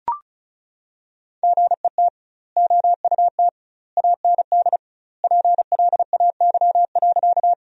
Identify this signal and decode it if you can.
Morse code, heard as a keyed tone, about 34 wpm, 700 Hz. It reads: GET OUT AND PLAY.
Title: 2x Word-Spacing - Sentences from Top 200 Words - 35wpm